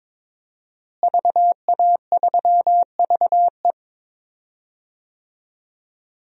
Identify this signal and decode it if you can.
Morse code, heard as a keyed tone, about 22 wpm, 700 Hz. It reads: VA3VE
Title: DXCC-100 Call Signs - 5 Characters - Repeated 3x - 22wpm